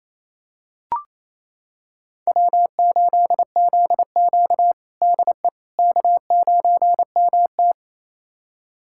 Morse code, heard as a keyed tone, about 28 wpm, 700 Hz. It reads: W8ZQ DE K9MT
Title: USA Callsign DE Callsign - 4 Characters - 28wpm